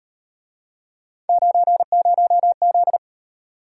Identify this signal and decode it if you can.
Morse code, heard as a keyed tone, about 38 wpm, 700 Hz. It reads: 90Z